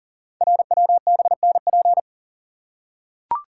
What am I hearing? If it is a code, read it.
Morse code, heard as a keyed tone, about 40 wpm, 700 Hz. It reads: RWBNP
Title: Code Groups - 5 Letters - 40wpm